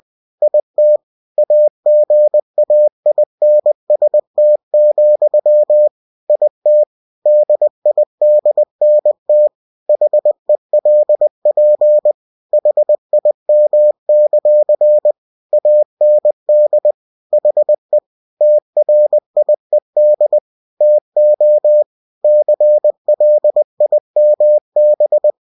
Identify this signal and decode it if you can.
Morse code, heard as a keyed tone, about 20 wpm, 600 Hz. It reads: IT AGAINST, IT DIDNT HELP HIM; AND HE TRIED TO CLIMB